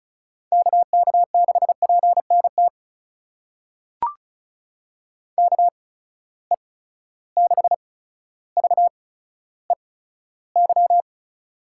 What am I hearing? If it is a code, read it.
Morse code, heard as a keyed tone, about 36 wpm, spaced as for 15 wpm, 700 Hz. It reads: KK6PNT KE6VEY
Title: USA Call Signs - 6 Characters - 8x Character Spacing - 35wpm